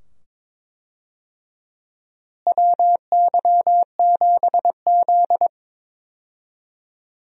Morse code, heard as a keyed tone, about 22 wpm, 700 Hz. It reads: WY7Z
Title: USA Call Signs - 4 Characters - Repeated 2x - 22wpm